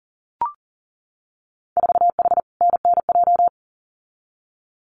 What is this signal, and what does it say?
Morse code, heard as a keyed tone, about 41 wpm, 700 Hz. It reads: 4H NNJ
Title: Winter Field Day Exchanges Encore 40wpm